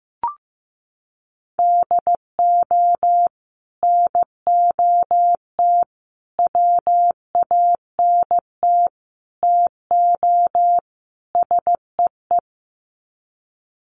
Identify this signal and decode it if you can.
Morse code, heard as a keyed tone, about 15 wpm, 700 Hz. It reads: DO NOT WANT TO SEE
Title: Sets of 5 Words 15wpm